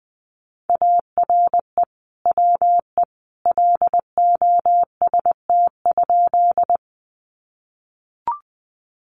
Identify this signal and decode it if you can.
Morse code, heard as a keyed tone, about 20 wpm, 700 Hz. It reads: ARE WE LOST?
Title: Sentences from Top 1400 Words 20wpm